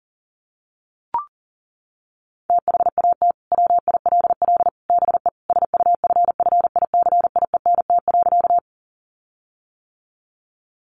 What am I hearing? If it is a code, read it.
Morse code, heard as a keyed tone, about 40 wpm, 700 Hz. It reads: THAT WILL BE SUFFICIENT.
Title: Sentences from Top 1400 Words 40wpm